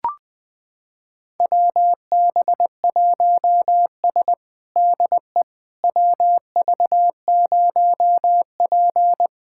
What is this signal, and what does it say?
Morse code, heard as a keyed tone, about 20 wpm, 700 Hz. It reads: WB1S DE WV0P
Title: USA Callsign DE Callsign - 4 Characters - 20wpm